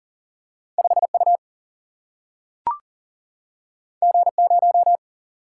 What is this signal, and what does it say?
Morse code, heard as a keyed tone, about 40 wpm, 700 Hz. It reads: 5 U G 0
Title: Single-Letter Single-Number 40wpm